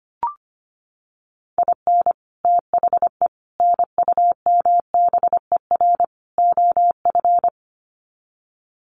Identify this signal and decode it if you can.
Morse code, heard as a keyed tone, about 25 wpm, 700 Hz. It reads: IN THE NUMBER OF
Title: Sets of 4 Words 25wpm